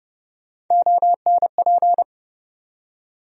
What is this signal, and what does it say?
Morse code, heard as a keyed tone, about 30 wpm, 700 Hz. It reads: ONP
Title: USA Call Sign Postfixes - 3 Letters 30wpm